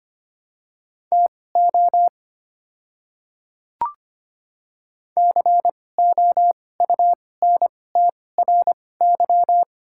Morse code, heard as a keyed tone, about 25 wpm, spaced as for 18 wpm, 700 Hz. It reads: TO COUNTRY
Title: 2x Character Spacing - Top 500 Words - 25wpm